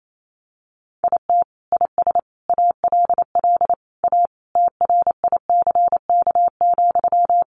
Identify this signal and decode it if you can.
Morse code, heard as a keyed tone, about 28 wpm, 700 Hz. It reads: IT IS ALL A TRICK,
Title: Sentences from Top 1500 Words 28wpm